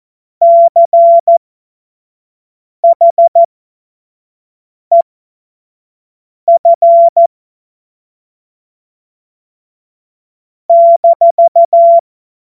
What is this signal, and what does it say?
Morse code, heard as a keyed tone, about 14 wpm, spaced as for 5 wpm, 680 Hz. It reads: CHEF -
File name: Morsekurs / Witze / Witz 25. Juli 2023 25 F70